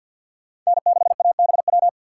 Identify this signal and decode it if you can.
Morse code, heard as a keyed tone, about 60 wpm, 700 Hz. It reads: N6ABW
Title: Speed Racing - USA 1x3 Call Signs 50wpm